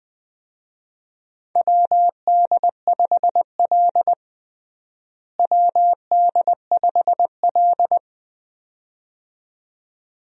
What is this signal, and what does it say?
Morse code, heard as a keyed tone, about 20 wpm, 700 Hz. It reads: WD5L WD5L
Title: USA Call Signs - 4 Characters - Repeated 2x - 20wpm